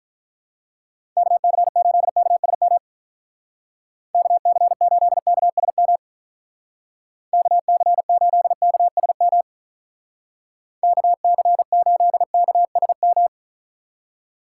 Morse code, heard as a keyed tone, about 44 wpm, 700 Hz. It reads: KC8KSM KC8KSM KC8KSM KC8KSM
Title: Speed Racing - USA Call Signs 35wpm